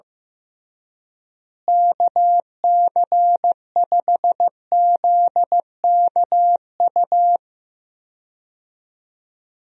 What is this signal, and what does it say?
Morse code, heard as a keyed tone, about 15 wpm, 700 Hz. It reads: KC5ZKU
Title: USA Call Signs - 6 Characters - Repeated 3x - 15wpm